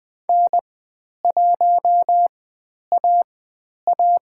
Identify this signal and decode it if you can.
Morse code, heard as a keyed tone, about 20 wpm, spaced as for 10 wpm, 700 Hz. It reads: N1AA
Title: 102 Lesson 17g - Letter u - Call Signs 20wpm